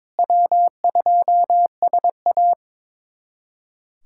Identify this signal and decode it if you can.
Morse code, heard as a keyed tone, about 22 wpm, 700 Hz. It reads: W2SA
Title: USA Callsign DE Callsign - 4 Characters - 22wpm